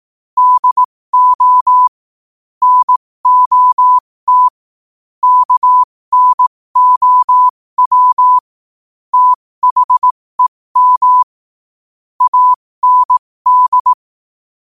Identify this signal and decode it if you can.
Morse code, heard as a keyed tone, about 18 wpm, spaced as for 15 wpm, 1000 Hz. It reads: DO NOT KNOW THEM AND
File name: qotdfiles / 15wpm